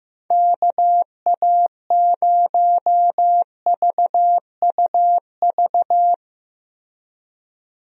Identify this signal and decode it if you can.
Morse code, heard as a keyed tone, about 15 wpm, 700 Hz. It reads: KA0VUV
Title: USA Call Signs - 6 Characters - Repeated 2x - 15wpm